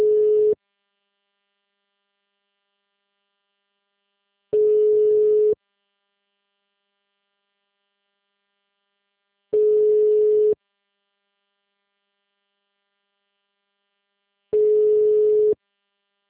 Ringback tone (Europe, 425 Hz).
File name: out- -2012- - - .11711